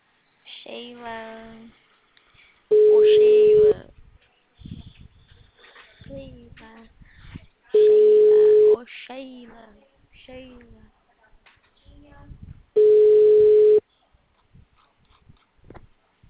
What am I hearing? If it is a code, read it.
Ringback tone (Europe, 425 Hz), with speech.